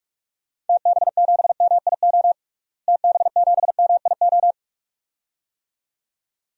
Morse code, heard as a keyed tone, about 46 wpm, 700 Hz. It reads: TB7MIO TB7MIO
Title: DXCC-100 Call Signs - 6 Characters - Repeated 2x - 45wpm